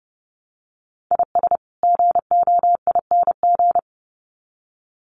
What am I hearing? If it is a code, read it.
Morse code, heard as a keyed tone, about 30 wpm, 700 Hz. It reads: IS GOING
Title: Sets of 2 Words 30wpm